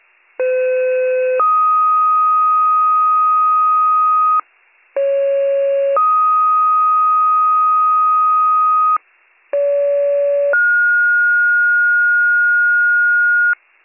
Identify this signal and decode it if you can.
Two-tone page, heard as a one-second tone, then a three-second tone, about 510 then 1250 Hz, about 550 then 1250 Hz, about 570 then 1430 Hz. It is heard from 0.4 to 13.5 s.